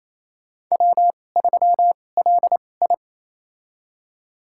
Morse code, heard as a keyed tone, about 28 wpm, spaced as for 20 wpm, 700 Hz. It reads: W3LI
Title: USA Call Signs - 4 Characters - 2x Character Spacing - 28wpm